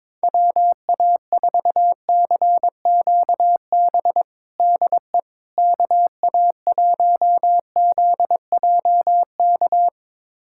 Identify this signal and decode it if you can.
Morse code, heard as a keyed tone, about 22 wpm, 700 Hz. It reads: WA4CQB DE KA1ZJK
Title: USA Callsign DE Callsign - 6 Characters - 22wpm